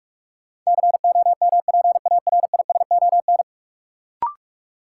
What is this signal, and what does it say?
Morse code, heard as a keyed tone, about 46 wpm, 700 Hz. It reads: COMPARISON
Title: Sentences from Top 1600 Words - New Word Drill - 45wpm